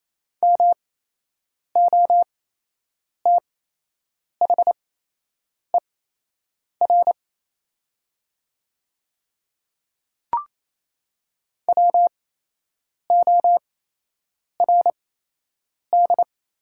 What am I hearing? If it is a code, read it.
Morse code, heard as a keyed tone, about 28 wpm, spaced as for 8 wpm, 700 Hz. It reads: MOTHER WORD
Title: 8x Character Spacing - Top 300 Words - 28wpm